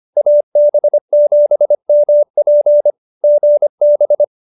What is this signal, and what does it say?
Morse code, heard as a keyed tone, about 25 wpm, 590 Hz. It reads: AB7MP GB